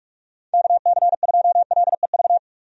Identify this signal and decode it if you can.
Morse code, heard as a keyed tone, about 46 wpm, 700 Hz. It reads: KC2LEV
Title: USA Call Signs - 6 Characters - 45wpm